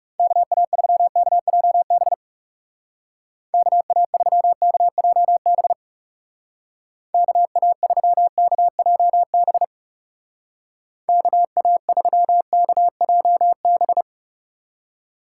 Morse code, heard as a keyed tone, about 40 wpm, 690 Hz. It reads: KA3KJB KA3KJB KA3KJB KA3KJB